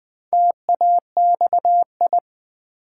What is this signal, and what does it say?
Morse code, heard as a keyed tone, about 20 wpm, 700 Hz. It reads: TAXI
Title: Build A Word - 1 Letter At A Time - 4-Letter Words - 20wpm